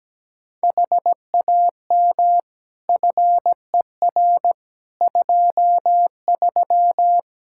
Morse code, heard as a keyed tone, about 17 wpm, 700 Hz. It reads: HAM FER 23
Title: Practice QSO - Exchange 3 - More Info - 17wpm